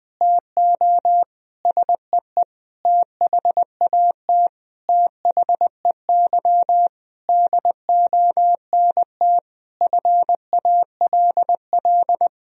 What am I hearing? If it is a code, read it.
Morse code, heard as a keyed tone, about 20 wpm, 700 Hz. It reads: TO SEE THAT THEY DONT FALL
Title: Sentences from Top 1000 Words - Review Part 2 - 20wpm